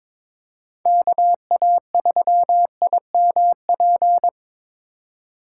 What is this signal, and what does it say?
Morse code, heard as a keyed tone, about 21 wpm, 700 Hz. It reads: KA3IMP
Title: USA Call Signs - 6 Characters - 4x Character Spacing - 22wpm